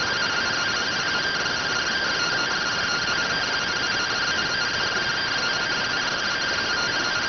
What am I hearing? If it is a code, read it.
Two-tone teleprinter signal, about 170 Hz shift.